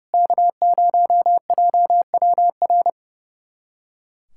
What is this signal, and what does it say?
Morse code, heard as a keyed tone, about 30 wpm, 700 Hz. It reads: K0JWR
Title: USA Callsign DE Callsign - 5 Characters - 30wpm